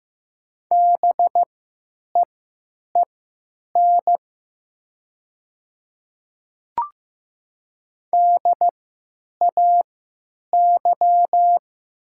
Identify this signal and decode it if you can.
Morse code, heard as a keyed tone, about 15 wpm, spaced as for 9 wpm, 700 Hz. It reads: BEEN DAY
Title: 3x Character Spacing - Top 100 Words - 15wpm